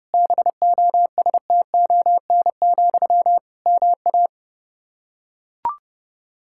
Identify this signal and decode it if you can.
Morse code, heard as a keyed tone, about 32 wpm, 700 Hz. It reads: BOSTON, MA
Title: Practice QSO - Exchange 1 - Beginning - 30wpm